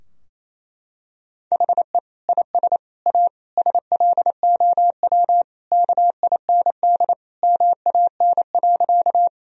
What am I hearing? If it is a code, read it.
Morse code, heard as a keyed tone, about 28 wpm, 700 Hz. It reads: HE IS A SLOW KIND MAN.
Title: Sentences from Top 1000 Words - Review Part 2 - 28wpm